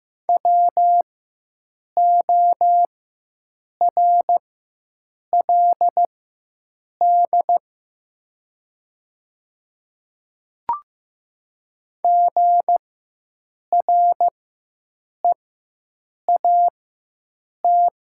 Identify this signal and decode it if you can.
Morse code, heard as a keyed tone, about 15 wpm, spaced as for 7 wpm, 700 Hz. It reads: WORLD GREAT
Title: 4x Character Spacing - Top 500 Words - 15wpm